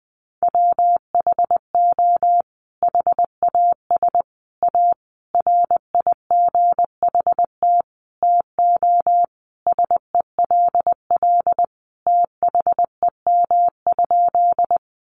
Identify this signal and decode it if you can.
Morse code, heard as a keyed tone, about 20 wpm, 700 Hz. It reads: WHO HAS A RIGHT TO SELL THEM?
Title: Sentences from Top 1100 Words 20wpm